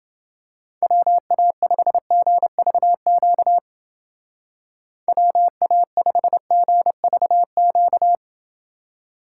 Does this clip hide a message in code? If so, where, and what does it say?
Yes: Morse code from 0.8 to 8.2 s, heard as a keyed tone, about 30 wpm, 700 Hz. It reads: WA5GVQ WA5GVQ